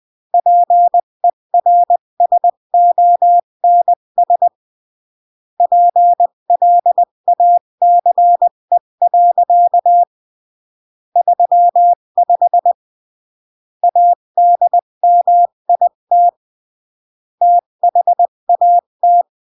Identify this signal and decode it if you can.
Morse code, heard as a keyed tone, about 20 wpm, 700 Hz. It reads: PERSONS PLACE. 35 ADMIT THAT
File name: CW / jodorowsky / 82 commandments 20-15